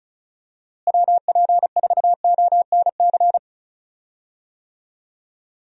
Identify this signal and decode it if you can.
Morse code, heard as a keyed tone, about 34 wpm, 700 Hz. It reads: WP4ONC
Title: USA Call Signs - 6 Characters - Repeated 3x - 35wpm